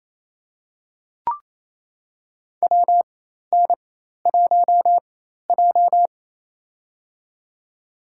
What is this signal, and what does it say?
Morse code, heard as a keyed tone, about 28 wpm, spaced as for 13 wpm, 700 Hz. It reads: WN1J
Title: USA Call Signs - 4 Characters - 4x Character Spacing - 28wpm